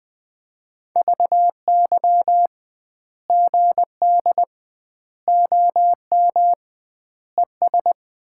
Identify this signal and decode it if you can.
Morse code, heard as a keyed tone, about 21 wpm, 700 Hz. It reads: VY GD OM ES
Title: Common QSO Phrases - 2x Word-Spacing - 20wpm